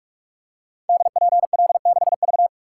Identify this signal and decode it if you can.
Morse code, heard as a keyed tone, about 46 wpm, 700 Hz. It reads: DPL6V
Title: Code Groups - 5 Letter:Numbers - 45wpm